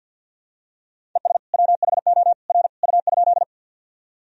Morse code, heard as a keyed tone, about 48 wpm, 700 Hz. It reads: ES WHO R U?
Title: Sentences from Top 100 Words and Abbreviations 50wpm